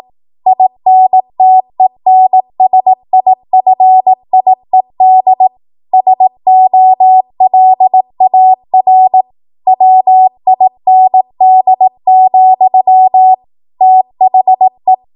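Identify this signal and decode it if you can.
Morse code, heard as a keyed tone, about 18 wpm, 750 Hz. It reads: INTENSIFIED SOLAR WIND, THE
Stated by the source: ARRL 18 wpm Morse code practice archive transmission.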